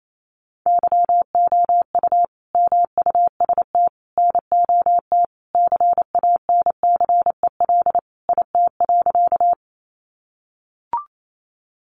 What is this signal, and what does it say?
Morse code, heard as a keyed tone, about 28 wpm, 700 Hz. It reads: YOU MUST NOT CANCEL IT.
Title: Sentences from Top 1200 Words 28wpm